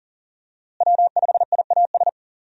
Morse code, heard as a keyed tone, about 40 wpm, 700 Hz. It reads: W5IAS
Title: USA Call Signs - 5 Characters - Repeated 3x - 40wpm